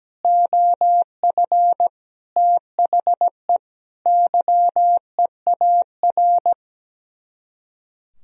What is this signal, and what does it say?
Morse code, heard as a keyed tone, about 17 wpm, 700 Hz. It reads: OF THE YEAR
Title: Speed-Racing - Sets of 5 Words 17wpm